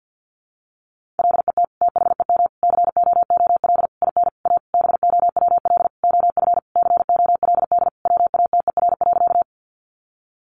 Morse code, heard as a keyed tone, about 50 wpm, 700 Hz. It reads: LET THEM COOL IN A BOWL OF COLD WATER.
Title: Sentences from Top 1200 Words 50wpm